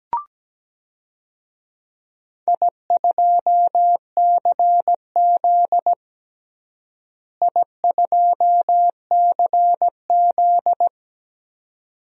Morse code, heard as a keyed tone, about 17 wpm, 700 Hz. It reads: I2CZ I2CZ
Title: DXCC-100 Call Signs - 4 Characters - Repeated 2x - 17wpm